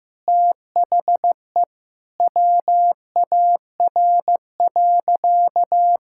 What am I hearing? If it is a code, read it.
Morse code, heard as a keyed tone, about 14 wpm, 700 Hz. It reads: THE WAR.
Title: Sentences from Top 1000 Words - Review Part 6 - 15wpm